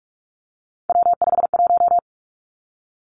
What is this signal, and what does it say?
Morse code, heard as a keyed tone, about 46 wpm, 700 Hz. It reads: W51
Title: Three Challenging Letter-Number Combinations 45wpm